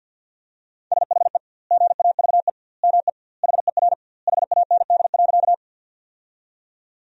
Morse code, heard as a keyed tone, about 46 wpm, 700 Hz. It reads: SHE GAVE ME HER HAND.
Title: 2x Word-Spacing - Sentences from Top 500 Words - 50wpm